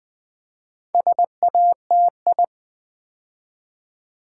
Morse code, heard as a keyed tone, about 20 wpm, 700 Hz. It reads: SATI
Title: Build A Word - 4 Letters At A Time - 12-Letter Words - 20wpm